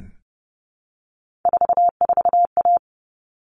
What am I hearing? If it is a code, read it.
Morse code, heard as a keyed tone, about 30 wpm, 700 Hz. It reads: 44A
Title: All Cut Numbers - 3-Digit 30wpm